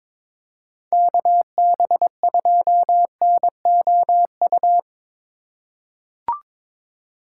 Morse code, heard as a keyed tone, about 21 wpm, 700 Hz. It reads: KB2NOU
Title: USA Call Signs - 3x Character Spacing - 22wpm